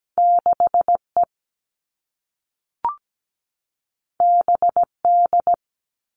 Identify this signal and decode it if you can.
Morse code, heard as a keyed tone, about 17 wpm, 700 Hz. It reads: 6E BD
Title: All Cut Numbers - 2-Digit 17wpm